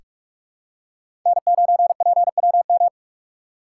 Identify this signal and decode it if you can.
Morse code, heard as a keyed tone, about 41 wpm, 700 Hz. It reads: N9PWM
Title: USA Call Signs - 2x Character Spacing - 45wpm